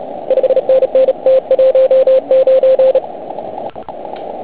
Morse code, heard as a keyed tone, about 30 wpm, 530 Hz. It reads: 5NN T19
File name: Archiv / CUC149 / OK1IF CUC149